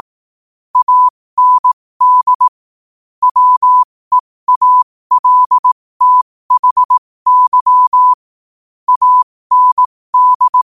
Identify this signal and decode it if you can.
Morse code, heard as a keyed tone, about 18 wpm, spaced as for 15 wpm, 1000 Hz. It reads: AND WEALTHY AND
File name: qotdfiles / 15wpm